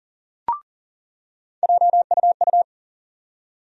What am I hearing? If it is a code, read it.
Morse code, heard as a keyed tone, about 40 wpm, 700 Hz. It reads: JUU
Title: USA Call Sign Postfixes - 3 Letters 40wpm